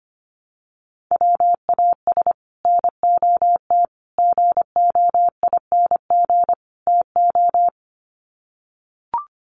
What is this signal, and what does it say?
Morse code, heard as a keyed tone, about 25 wpm, 700 Hz. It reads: WAS NOT GOING TO
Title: Sets of 4 Words 25wpm